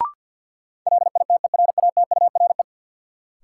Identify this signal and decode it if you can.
Morse code, heard as a keyed tone, about 48 wpm, 700 Hz. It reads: LITERATURE
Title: Sentences from Top 600 Words - New Word Drill - 50wpm